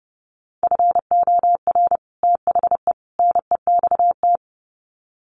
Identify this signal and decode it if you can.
Morse code, heard as a keyed tone, about 30 wpm, 700 Hz. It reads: FOR THE NEXT
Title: Sets of 3 Words 30wpm